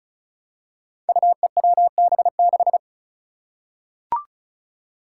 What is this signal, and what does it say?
Morse code, heard as a keyed tone, about 34 wpm, 700 Hz. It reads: UEWB6
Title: Five Challenging Letter-Number Combinations 35wpm